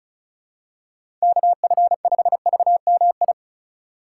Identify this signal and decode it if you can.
Morse code, heard as a keyed tone, about 36 wpm, 700 Hz. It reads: KF5VMI
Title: USA Call Signs - 6 Characters - 2x Character Spacing - 35wpm